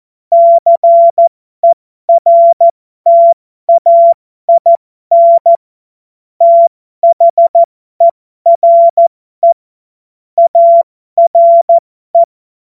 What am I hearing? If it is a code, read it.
Morse code, heard as a keyed tone, about 14 wpm, spaced as for 12 wpm, 680 Hz. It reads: CERTAIN THERE ARE
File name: Morsekurs / Witze